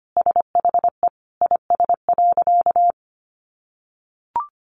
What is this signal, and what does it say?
Morse code, heard as a keyed tone, about 25 wpm, 700 Hz. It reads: SHE IS.
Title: Sentences from Top 1400 Words 25wpm